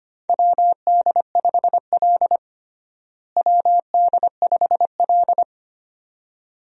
Morse code, heard as a keyed tone, about 25 wpm, 700 Hz. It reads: WD5L WD5L
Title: USA Call Signs - 4 Characters - Repeated 2x - 25wpm